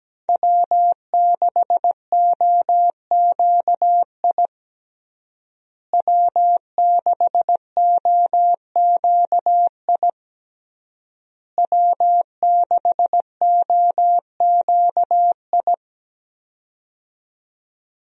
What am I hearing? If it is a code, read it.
Morse code, heard as a keyed tone, about 17 wpm, 700 Hz. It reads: W6OQI W6OQI W6OQI
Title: USA Call Signs - 5 Characters - Repeated 3x - 17wpm